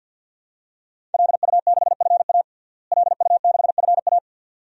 Morse code, heard as a keyed tone, about 50 wpm, 700 Hz. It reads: LU6FA LU6FA
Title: DXCC-100 Call Signs - 5 Characters - Repeated 2x - 50wpm